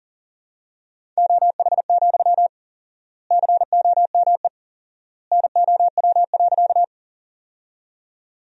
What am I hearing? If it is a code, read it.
Morse code, heard as a keyed tone, about 40 wpm, 700 Hz. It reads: OH, COME NOW.
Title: 4x Word-Spacing - Sentences from Top 500 Words - 40wpm